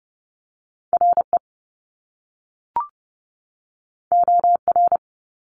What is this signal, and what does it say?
Morse code, heard as a keyed tone, about 30 wpm, 700 Hz. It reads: RE OR
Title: Top Two Letter Combinations in Words 30wpm